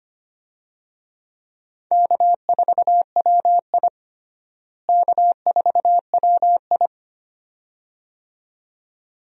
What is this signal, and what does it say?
Morse code, heard as a keyed tone, about 25 wpm, 700 Hz. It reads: K4WI K4WI
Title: USA Call Signs - 4 Characters - Repeated 2x - 25wpm